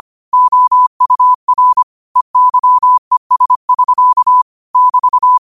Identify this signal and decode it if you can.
Morse code, heard as a keyed tone, about 25 wpm, 1000 Hz. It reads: OUR EYES <SK> <BT>
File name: qotdfiles / 25wpm